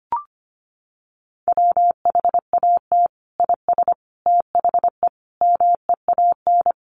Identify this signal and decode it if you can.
Morse code, heard as a keyed tone, about 25 wpm, 700 Hz. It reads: WHAT IS THE MEAN
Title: Sentences from Top 900 Words 25wpm